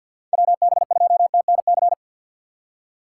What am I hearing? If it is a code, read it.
Morse code, heard as a keyed tone, about 48 wpm, 700 Hz. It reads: WB2TNC
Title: USA Call Signs - 6 Characters - 4x Character Spacing - 50wpm